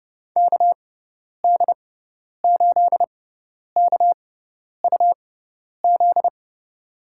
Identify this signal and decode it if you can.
Morse code, heard as a keyed tone, about 32 wpm, spaced as for 10 wpm, 700 Hz. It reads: KD8KUZ
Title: USA Call Signs - 6 Characters - 6x Character Spacing - 30wpm